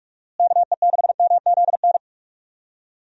Morse code, heard as a keyed tone, about 43 wpm, 700 Hz. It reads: KE6MZN
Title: USA Call Signs - 6 Characters - Repeated 3x - 45wpm